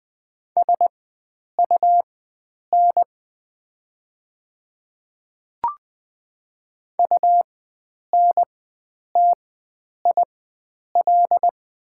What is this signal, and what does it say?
Morse code, heard as a keyed tone, about 20 wpm, spaced as for 9 wpm, 700 Hz. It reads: SUN UNTIL
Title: 4x Character Spacing - Top 300 Words - 20wpm